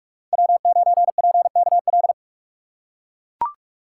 Morse code, heard as a keyed tone, about 42 wpm, 700 Hz. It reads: W9PKL
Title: USA Call Signs - 5 Characters - 6x Character Spacing - 45wpm